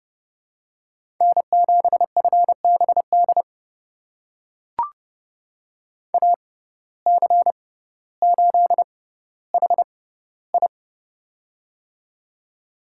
Morse code, heard as a keyed tone, about 32 wpm, spaced as for 16 wpm, 700 Hz. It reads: N7FBD AC8HI
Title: USA Call Signs - 5 Characters - 6x Character Spacing - 30wpm